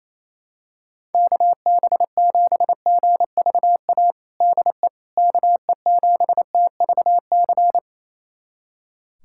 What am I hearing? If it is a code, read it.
Morse code, heard as a keyed tone, about 28 wpm, 700 Hz. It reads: KB7GVA DE KE7TVC
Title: USA Callsign DE Callsign - 6 Characters - 28wpm